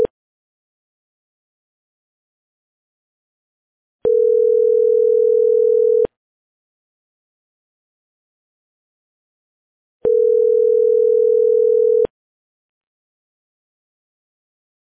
Ringback tone (North America), 440 + 480 Hz.